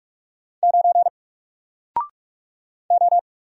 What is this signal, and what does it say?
Morse code, heard as a keyed tone, about 46 wpm, 700 Hz. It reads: 9 O
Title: Single Letter-Number 45wpm